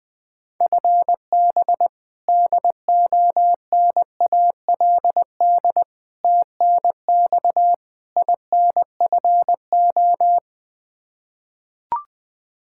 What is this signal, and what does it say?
Morse code, heard as a keyed tone, about 20 wpm, 700 Hz. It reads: FB DONALD TNX INFO
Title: Practice QSO - Exchange 3 - More Info - 20wpm